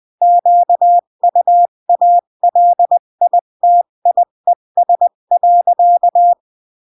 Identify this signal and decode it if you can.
Morse code, heard as a keyed tone, about 20 wpm, 700 Hz. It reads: QUALITIES.
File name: CW / jodorowsky / 82 commandments 20-15